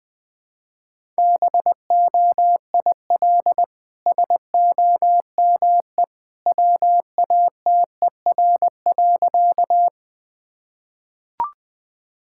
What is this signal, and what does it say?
Morse code, heard as a keyed tone, about 20 wpm, 700 Hz. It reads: BOIL SOME WATER.
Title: Easy Sentences with Hints - 20wpm